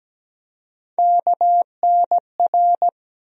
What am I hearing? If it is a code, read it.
Morse code, heard as a keyed tone, about 17 wpm, 700 Hz. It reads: KNR
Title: USA Call Sign Postfixes - 3 Letters 17wpm